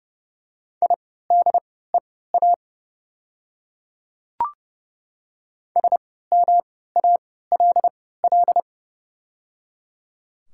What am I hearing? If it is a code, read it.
Morse code, heard as a keyed tone, about 30 wpm, spaced as for 17 wpm, 700 Hz. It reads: IDEA SMALL
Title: 3x Character Spacing - Top 500 Words - 30wpm